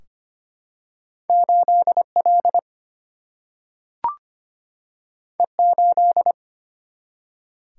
Morse code, heard as a keyed tone, about 25 wpm, 700 Hz. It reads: 8L E8
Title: Single-Letter Single-Number 25wpm